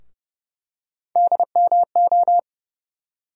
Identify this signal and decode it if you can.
Morse code, heard as a keyed tone, about 30 wpm, 700 Hz. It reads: DMO
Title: USA Call Sign Postfixes - 3 Letters 30wpm